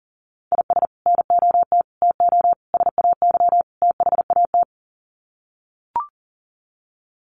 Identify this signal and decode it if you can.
Morse code, heard as a keyed tone, about 40 wpm, 700 Hz. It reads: IS NOT TO SAY THAT
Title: Sets of 5 Words 40wpm